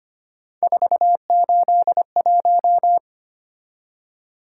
Morse code, heard as a keyed tone, about 25 wpm, 700 Hz. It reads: 481
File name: Three Letter-Number Combinations - 25wpm